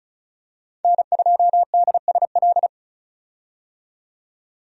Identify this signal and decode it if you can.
Morse code, heard as a keyed tone, about 34 wpm, 700 Hz. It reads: N2DSL
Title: USA Call Signs - 5 Characters - 35wpm